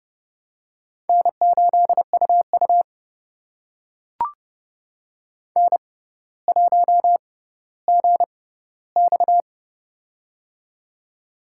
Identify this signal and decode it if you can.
Morse code, heard as a keyed tone, about 30 wpm, spaced as for 15 wpm, 700 Hz. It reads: N8UU N1GX
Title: USA Call Signs - 4 Characters - 6x Character Spacing - 30wpm